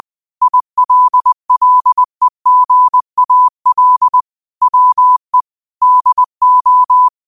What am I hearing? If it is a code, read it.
Morse code, heard as a keyed tone, about 20 wpm, 1000 Hz. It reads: ILLEGAL WE DO